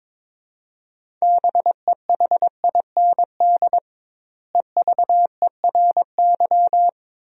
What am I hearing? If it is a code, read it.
Morse code, heard as a keyed tone, about 22 wpm, 700 Hz. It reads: BEHIND EVERY
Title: 2x Word-Spacing - Sentences from Top 500 Words - 22wpm